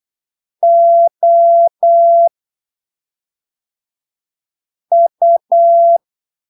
Morse code, heard as a keyed tone, about 8 wpm, spaced as for 3 wpm, 680 Hz. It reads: OU
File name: arquivos / texto / texto 28 08ppm